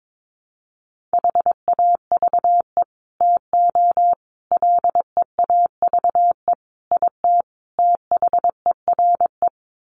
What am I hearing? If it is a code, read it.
Morse code, heard as a keyed tone, about 22 wpm, 700 Hz. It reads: HAVE TO LEAVE IT THERE
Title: Sets of 5 Words Encore2 22wpm